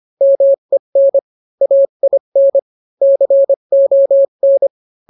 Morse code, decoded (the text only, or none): MEN AIN CON